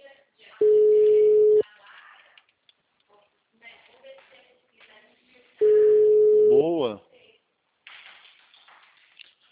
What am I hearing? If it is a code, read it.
Ringback tone (Europe, 425 Hz), with speech.
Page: Bauru - SP